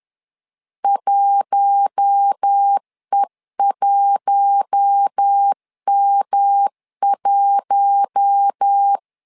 Morse code, heard as a keyed tone, about 10 wpm, 800 Hz. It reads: 1E1M1
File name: LEZIONI / 4 Q1 10wpm 2448